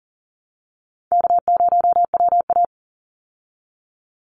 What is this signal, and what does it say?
Morse code, heard as a keyed tone, about 40 wpm, 700 Hz. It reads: K0WA
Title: Call Signs Repeated 3x - 40wpm